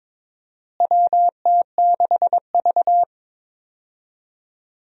Morse code, heard as a keyed tone, about 22 wpm, 700 Hz. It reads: WT6V
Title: USA Call Signs - 4 Characters - Repeated 3x - 22wpm